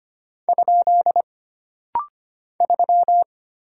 Morse code, heard as a keyed tone, about 25 wpm, 700 Hz. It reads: ? 3
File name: Single Character (Letter:Number:Punctuation:Prosign) - Rapid-Fire - 25wpm